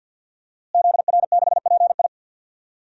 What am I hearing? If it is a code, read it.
Morse code, heard as a keyed tone, about 53 wpm, 700 Hz. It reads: ZR6PI
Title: DXCC-100 Call Signs - 5 Characters - Repeated 2x - 50wpm